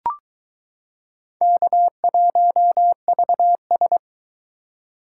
Morse code, heard as a keyed tone, about 22 wpm, 700 Hz. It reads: K1VS